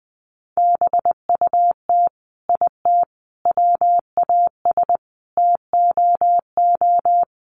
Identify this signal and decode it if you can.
Morse code, heard as a keyed tone, about 20 wpm, 700 Hz. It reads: BUT IT WAS TOO
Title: Sets of 5 Words Encore2 20wpm